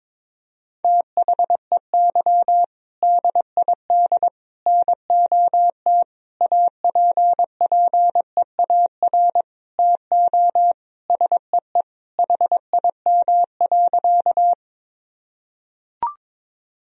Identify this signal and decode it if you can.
Morse code, heard as a keyed tone, about 22 wpm, 700 Hz. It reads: THEY DID NOT APPEAR TO SEE HIM.
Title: Sentences from Top 1000 Words - Review Part 2 - 22wpm